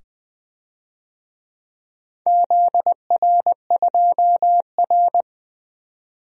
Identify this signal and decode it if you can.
Morse code, heard as a keyed tone, about 20 wpm, 700 Hz. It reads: ZR2R
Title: DXCC-100 Call Signs - 4 Characters - 20wpm